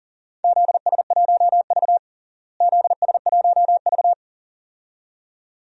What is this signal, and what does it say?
Morse code, heard as a keyed tone, about 37 wpm, 700 Hz. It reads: ZS1V ZS1V
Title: DXCC-100 Call Signs - 4 Characters - Repeated 2x - 40wpm